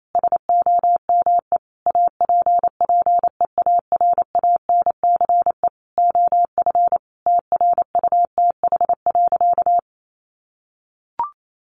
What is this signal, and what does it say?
Morse code, heard as a keyed tone, about 28 wpm, 700 Hz. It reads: SOME APPEARANCE OF TRUTH.
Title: Sentences from Top 1100 Words 28wpm